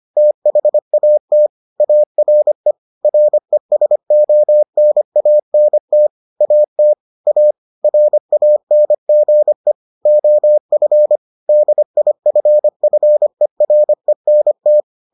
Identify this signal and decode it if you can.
Morse code, heard as a keyed tone, about 25 wpm, 600 Hz. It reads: THAT ARE RESONANT AT A RANGE OF DIFFERENT